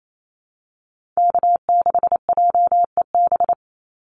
Morse code, heard as a keyed tone, about 28 wpm, 700 Hz. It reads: K6JEB
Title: Call Signs - 28wpm